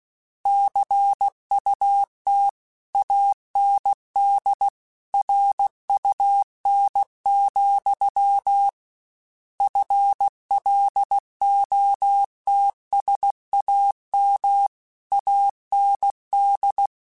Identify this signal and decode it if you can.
Morse code, heard as a keyed tone, about 16 wpm, 800 Hz. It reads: CUT AND RUN, FLOTSAM AND
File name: Misc / Misc-mp3s / Word Pairs @15